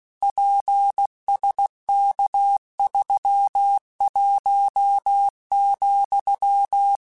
Morse code, heard as a keyed tone, about 16 wpm, 800 Hz. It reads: PSK31,
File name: QSOs / QSOs-mp3s / WA7QOX 2@15